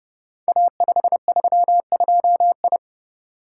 Morse code, heard as a keyed tone, about 30 wpm, 700 Hz. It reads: A532I
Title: Code Groups - 5 Letter:Numbers - 30wpm